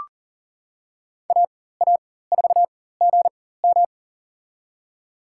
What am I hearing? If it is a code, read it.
Morse code, heard as a keyed tone, about 40 wpm, spaced as for 19 wpm, 700 Hz. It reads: AA4GM
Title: USA Call Signs - 5 Characters - 4x Character Spacing - 40wpm